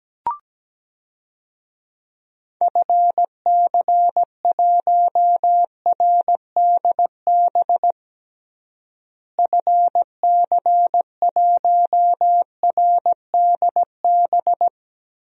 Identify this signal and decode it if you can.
Morse code, heard as a keyed tone, about 17 wpm, 700 Hz. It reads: FC1RDB FC1RDB
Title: DXCC-100 Call Signs - 6 Characters - Repeated 3x - 17wpm